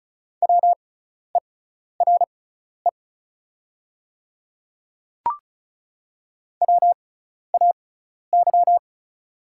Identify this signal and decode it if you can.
Morse code, heard as a keyed tone, about 36 wpm, spaced as for 12 wpm, 700 Hz. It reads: WERE WAY
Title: 6x Character Spacing - Top 100 Words - 35wpm